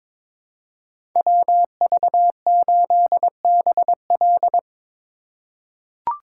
Morse code, heard as a keyed tone, about 22 wpm, 700 Hz. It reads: WV8BL
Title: USA Call Signs - 8x Character Spacing - 22wpm